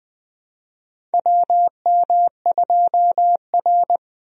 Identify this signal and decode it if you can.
Morse code, heard as a keyed tone, about 20 wpm, 700 Hz. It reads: WM2R
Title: USA Call Signs - 4 Characters - Repeated 3x - 20wpm